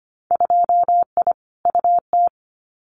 Morse code, heard as a keyed tone, about 25 wpm, 700 Hz. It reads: 2I UT
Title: Winter Field Day Exchanges 25wpm